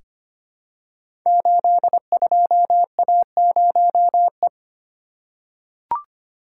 Morse code, heard as a keyed tone, about 25 wpm, 700 Hz. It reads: 82A0E
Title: Code Groups - 5 Letter:Numbers - 25wpm